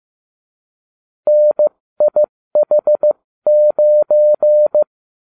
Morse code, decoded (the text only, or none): NIH9